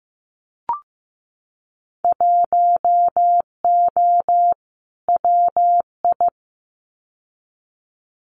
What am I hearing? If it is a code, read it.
Morse code, heard as a keyed tone, about 15 wpm, 700 Hz. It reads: 1O WI
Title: Winter Field Day Exchanges 15wpm